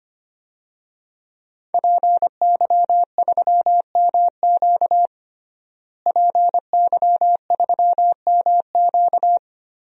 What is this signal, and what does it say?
Morse code, heard as a keyed tone, about 25 wpm, 700 Hz. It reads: PY3MQ PY3MQ
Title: DXCC-100 Call Signs - 5 Characters - Repeated 2x - 25wpm